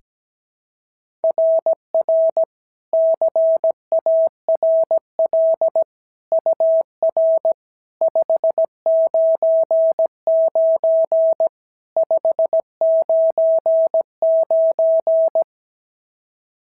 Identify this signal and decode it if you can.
Morse code, heard as a keyed tone, about 17 wpm, 650 Hz. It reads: RR CARL UR 599 599
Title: SOTA QSO Practice 17wpm